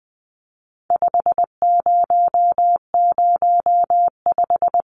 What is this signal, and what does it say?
Morse code, heard as a keyed tone, about 20 wpm, 700 Hz. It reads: 5005
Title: Numbers - 4-Digit 20wpm